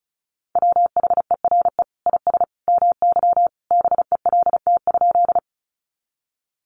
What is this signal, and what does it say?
Morse code, heard as a keyed tone, about 34 wpm, 700 Hz. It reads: WHERE IS MY BELT?
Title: Sentences from Top 1400 Words 35wpm